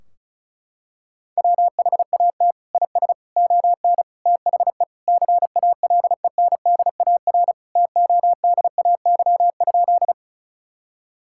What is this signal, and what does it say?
Morse code, heard as a keyed tone, about 36 wpm, 700 Hz. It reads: WHAT IS ON THE CALENDAR TODAY?
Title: Sentences from Top 1600 Words 35wpm